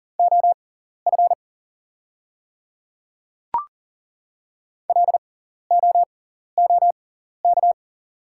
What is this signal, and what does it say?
Morse code, heard as a keyed tone, about 40 wpm, spaced as for 14 wpm, 690 Hz. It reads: OF LOOK
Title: 6x Character Spacing - Top 100 Words - 40wpm